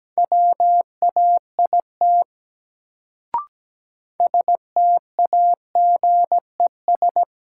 Morse code, heard as a keyed tone, about 17 wpm, 700 Hz. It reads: WAIT STAGES
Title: Sets of 5 Words Encore - New Word Drill - 17wpm